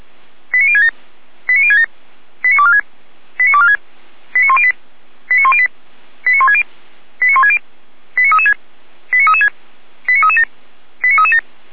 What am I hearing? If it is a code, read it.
Five-tone selective call (ZVEI), heard as quick runs of five tones, tones of about 70 ms. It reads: 89067 89067 89246 89246 89108 89108 89160 89160 89336 89336 89337 89337
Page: /455/SDR/ZVEI/89072/